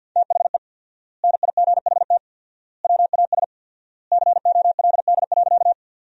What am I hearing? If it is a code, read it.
Morse code, heard as a keyed tone, about 48 wpm, 700 Hz. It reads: THE NIGHT WAS COLD.
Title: 4x Word-Spacing - Sentences from Top 400 Words - 50wpm